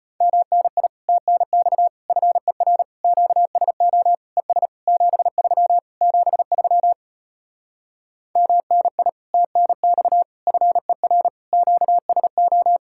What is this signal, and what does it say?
Morse code, heard as a keyed tone, about 33 wpm, 700 Hz. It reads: MNI TNX FER QSO ES 73 73 MNI TNX FER QSO